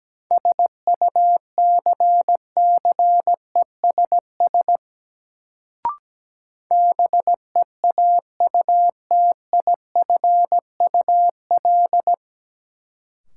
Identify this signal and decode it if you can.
Morse code, heard as a keyed tone, about 17 wpm, 700 Hz. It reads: SUCCESS BEAUTIFUL
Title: Sentences from Top 900 Words - New Word Drill - 17wpm